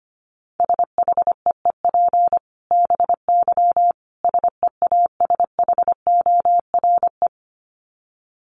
Morse code, heard as a keyed tone, about 25 wpm, 700 Hz. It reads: SHEEP BY SEASHORE
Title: Dit Challenge 25wpm